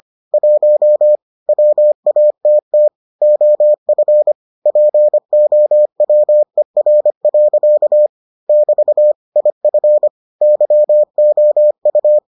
Morse code, heard as a keyed tone, about 25 wpm, 600 Hz. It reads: 1 WATT OF POWER. <BT> IF YOU